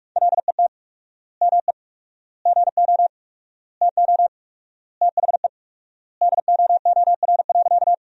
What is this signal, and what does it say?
Morse code, heard as a keyed tone, about 46 wpm, 700 Hz. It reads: LET ME GO TO THE DOOR.
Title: 4x Word-Spacing - Sentences from Top 300 Words - 45wpm